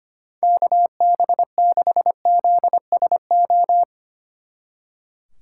Morse code, heard as a keyed tone, about 32 wpm, 700 Hz. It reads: KB6ZSO